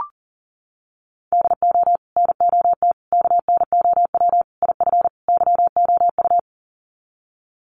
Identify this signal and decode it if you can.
Morse code, heard as a keyed tone, about 40 wpm, 700 Hz. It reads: DO NOT KNOW IF YOU
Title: Sets of 5 Words 40wpm